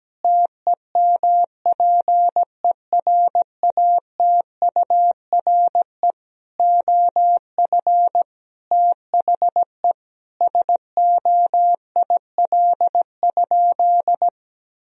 Morse code, heard as a keyed tone, about 17 wpm, 700 Hz. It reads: TEMPERATURE OF THE SOIL?
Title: Sentences from Top 1000 Words - Review Part 4 - 17wpm